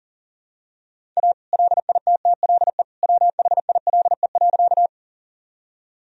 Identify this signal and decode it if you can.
Morse code, heard as a keyed tone, about 40 wpm, 700 Hz. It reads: A LITTLE WHILE.
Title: Sentences from Top 1000 Words - Review Part 1 - 40wpm